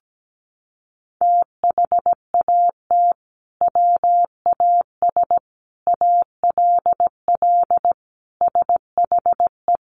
Morse code, heard as a keyed tone, about 17 wpm, 700 Hz. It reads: THAT WAS ALL SHE
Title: Sentences from Top 100 Words 17wpm